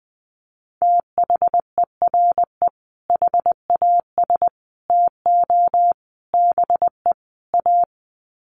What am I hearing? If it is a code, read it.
Morse code, heard as a keyed tone, about 20 wpm, 700 Hz. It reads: THERE HAS TO BE A
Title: Sets of 5 Words Encore 20wpm